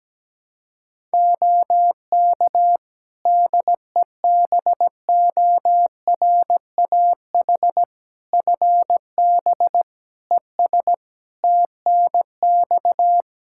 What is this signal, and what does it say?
Morse code, heard as a keyed tone, about 17 wpm, 700 Hz. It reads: OK DEBORAH FB ES TNX
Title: Practice QSO - Exchange 2 - Info - 17wpm